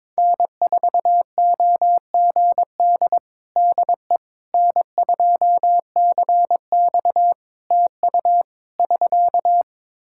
Morse code, heard as a keyed tone, about 22 wpm, 700 Hz. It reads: N4OGD DE N2CX TU <SK>
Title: Practice QSO - Exchange 4 - Ending - 22wpm